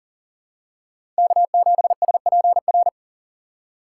Morse code, heard as a keyed tone, about 40 wpm, 700 Hz. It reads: K7SPR
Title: USA Call Signs - 5 Characters - 3x Character Spacing - 40wpm